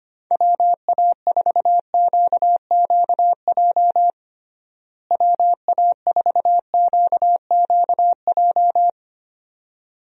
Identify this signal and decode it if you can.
Morse code, heard as a keyed tone, about 25 wpm, 700 Hz. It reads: WA4QQJ WA4QQJ
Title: USA Call Signs - 6 Characters - Repeated 2x - 25wpm